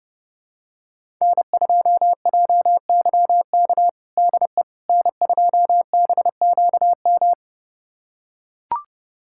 Morse code, heard as a keyed tone, about 30 wpm, 700 Hz. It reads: N2JYK DE N2BQM
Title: USA Callsign DE Callsign - 5 Characters - 30wpm